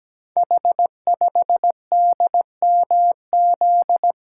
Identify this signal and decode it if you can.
Morse code, heard as a keyed tone, about 17 wpm, 690 Hz. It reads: H5DMZ
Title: Code Groups - 5 Letter:Numbers - 17wpm